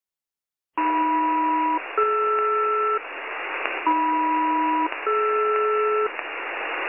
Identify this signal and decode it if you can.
Aircraft SELCAL, heard as two pairs of tones. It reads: AM-DR AM-DR